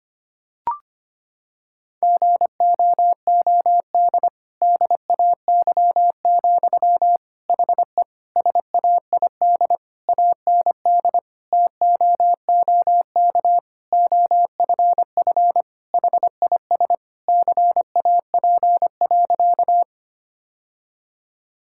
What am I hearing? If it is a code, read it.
Morse code, heard as a keyed tone, about 25 wpm, 700 Hz. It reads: GOOD DAY, HE SAID AND TOOK OFF HIS CAP.
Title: Sentences from Top 1600 Words 25wpm